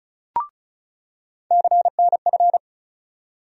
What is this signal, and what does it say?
Morse code, heard as a keyed tone, about 34 wpm, 700 Hz. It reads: CNF
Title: USA Call Sign Postfixes - 3 Letters 35wpm